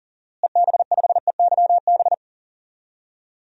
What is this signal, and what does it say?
Morse code, heard as a keyed tone, about 40 wpm, 700 Hz. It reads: EB5EYB
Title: DXCC-100 Call Signs - 6 Characters - Repeated 2x - 40wpm